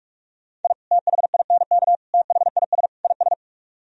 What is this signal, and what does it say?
Morse code, heard as a keyed tone, about 56 wpm, 700 Hz. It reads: I THINK THIS IS
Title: Speed-Racing - Sets of 4 Words 45wpm